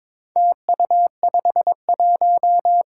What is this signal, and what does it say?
Morse code, heard as a keyed tone, about 22 wpm, 700 Hz. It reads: TU51
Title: Four Challenging Letter-Number Combinations 22wpm